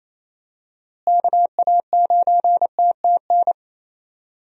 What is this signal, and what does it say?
Morse code, heard as a keyed tone, about 28 wpm, 700 Hz. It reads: KA9TTN
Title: USA Call Signs - 6 Characters - Repeated 2x - 28wpm